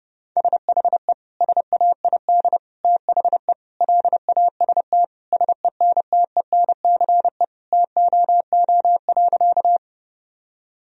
Morse code, heard as a keyed tone, about 32 wpm, 700 Hz. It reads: SHE SAID THE LAST SENTENCE TOO.
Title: Sentences from Top 1000 Words - Review Part 1 - 30wpm